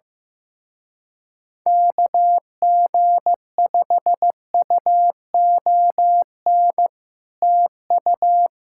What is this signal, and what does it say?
Morse code, heard as a keyed tone, about 15 wpm, 700 Hz. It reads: KG5UON TU
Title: Speed-Racing Common QSO Phrases Encore - 15wpm